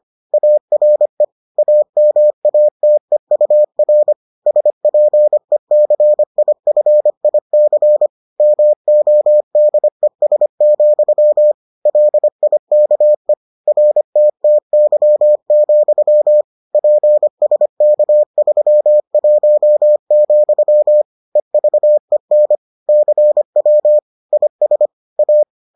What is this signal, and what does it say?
Morse code, heard as a keyed tone, about 25 wpm, 600 Hz. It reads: ARE AMATEUR SPECIFIC MODES, LIKE RTTY, PSK31, EVEN CW IS A